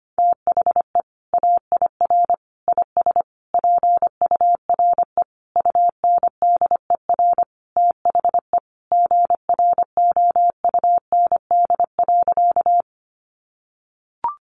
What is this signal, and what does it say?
Morse code, heard as a keyed tone, about 25 wpm, 700 Hz. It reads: THE AIR IS PURE UNDER THE GROUND.
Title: Sentences from Top 1500 Words 25wpm